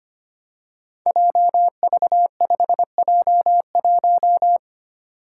Morse code, heard as a keyed tone, about 25 wpm, 700 Hz. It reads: JV5J1
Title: Five Challenging Letter-Number Combinations 25wpm